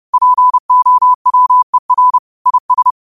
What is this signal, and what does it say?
Morse code, heard as a keyed tone, about 30 wpm, 1000 Hz. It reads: POWER IS
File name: qotdfiles / 30wpm